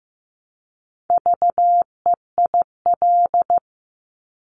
Morse code, heard as a keyed tone, about 15 wpm, 700 Hz. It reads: VEIL
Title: Multi-letter Phonemes Repeated 3x 15wpm